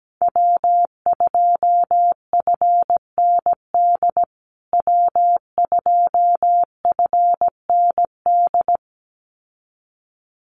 Morse code, heard as a keyed tone, about 17 wpm, 700 Hz. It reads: W2FND W2FND
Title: USA Call Signs Repeated 2x 17wpm